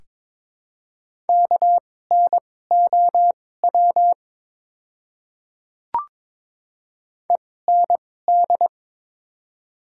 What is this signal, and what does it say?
Morse code, heard as a keyed tone, about 21 wpm, spaced as for 16 wpm, 700 Hz. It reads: KNOW END
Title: 2x Character Spacing - Top 200 Words - 22wpm